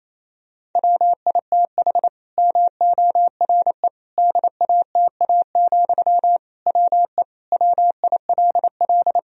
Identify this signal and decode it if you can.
Morse code, heard as a keyed tone, about 28 wpm, 700 Hz. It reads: WITH MORE DATA, WE WILL
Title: Sentences from Top 1000 Words - Review Part 3 - 28wpm